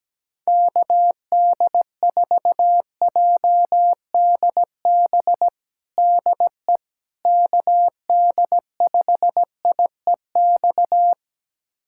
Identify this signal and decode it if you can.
Morse code, heard as a keyed tone, about 17 wpm, 700 Hz. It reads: KD4JDB DE KD5IEX
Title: USA Callsign DE Callsign 17wpm